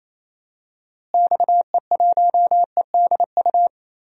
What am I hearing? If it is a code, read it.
Morse code, heard as a keyed tone, about 27 wpm, 700 Hz. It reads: XE1EDU
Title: DXCC-100 Call Signs - 6 Characters - Repeated 3x - 28wpm